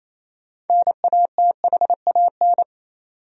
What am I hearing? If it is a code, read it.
Morse code, heard as a keyed tone, about 28 wpm, 700 Hz. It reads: NATHAN
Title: USA Common Names - 28wpm